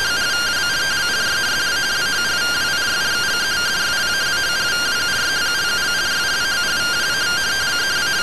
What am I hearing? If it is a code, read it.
Two-tone teleprinter signal, about 170 Hz shift.